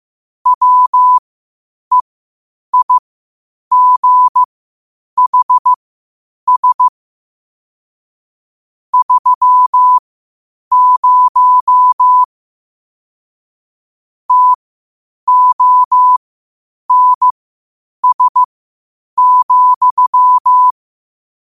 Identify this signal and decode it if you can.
Morse code, heard as a keyed tone, about 15 wpm, spaced as for 8 wpm, 1000 Hz. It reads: WEIGHS 30 TONS,
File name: qotdfiles / 8wpm